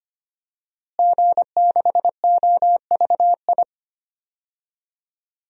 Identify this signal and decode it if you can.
Morse code, heard as a keyed tone, about 25 wpm, 700 Hz. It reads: G6OVI
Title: DXCC-100 Call Signs - 5 Characters - Repeated 2x - 25wpm